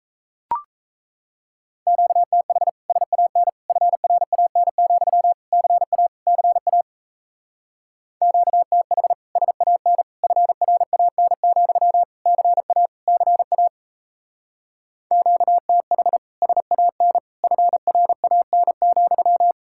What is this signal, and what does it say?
Morse code, heard as a keyed tone, about 36 wpm, 700 Hz. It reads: QTH SAN FRAN, CA CA QTH SAN FRAN, CA CA QTH SAN FRAN,